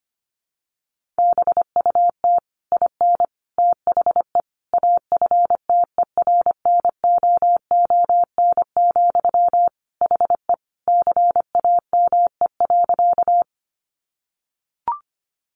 Morse code, heard as a keyed tone, about 25 wpm, 700 Hz. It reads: BUT IN THE AFTERNOON, HE CAME.
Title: Sentences from Top 1500 Words 25wpm